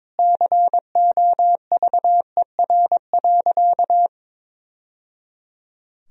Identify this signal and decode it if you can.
Morse code, heard as a keyed tone, about 22 wpm, 700 Hz. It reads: COVER.
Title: Sentences from Top 1000 Words - Review Part 1 - 22wpm